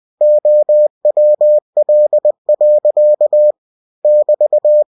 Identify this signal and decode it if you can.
Morse code, heard as a keyed tone, about 20 wpm, 600 Hz. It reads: OWL. <BT>
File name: cw-books / winnie-the-pooh / winnie-20wpm-page-0096